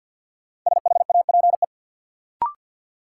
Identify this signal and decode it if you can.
Morse code, heard as a keyed tone, about 46 wpm, 700 Hz. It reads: SHAPE
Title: Sets of 5 Words Encore - New Word Drill - 50wpm